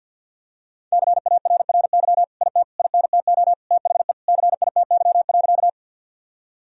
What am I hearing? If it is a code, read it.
Morse code, heard as a keyed tone, about 50 wpm, 700 Hz. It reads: CARRY IT INTO THE CITY.
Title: Sentences from Top 1000 Words - Review Part 1 - 50wpm